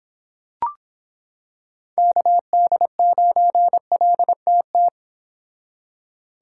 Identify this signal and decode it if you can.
Morse code, heard as a keyed tone, about 26 wpm, 700 Hz. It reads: KD9LTT